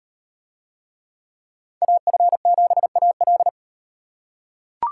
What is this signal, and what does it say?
Morse code, heard as a keyed tone, about 38 wpm, 700 Hz. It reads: AF7AL